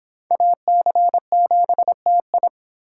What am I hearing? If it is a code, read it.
Morse code, heard as a keyed tone, about 24 wpm, 700 Hz. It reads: AC7TI